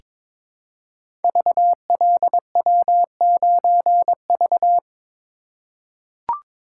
Morse code, heard as a keyed tone, about 22 wpm, 700 Hz. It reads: VLW9V
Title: Code Groups - 5 Letter:Numbers - 22wpm